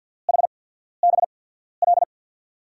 Morse code, heard as a keyed tone, about 53 wpm, spaced as for 14 wpm, 690 Hz. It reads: HBL
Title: USA Call Signs - 6 Characters - 8x Character Spacing - 50wpm